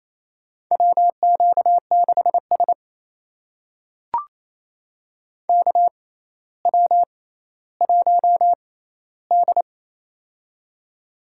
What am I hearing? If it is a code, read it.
Morse code, heard as a keyed tone, about 28 wpm, spaced as for 14 wpm, 700 Hz. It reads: WQ6S KW1D
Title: USA Call Signs - 4 Characters - 6x Character Spacing - 28wpm